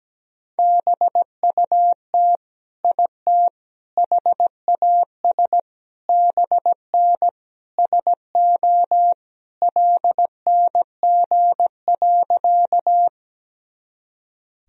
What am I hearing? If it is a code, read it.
Morse code, heard as a keyed tone, about 17 wpm, 700 Hz. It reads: BUT IT HAS BN SO LNG.
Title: Sentences from Top 100 Words and Abbreviations 17wpm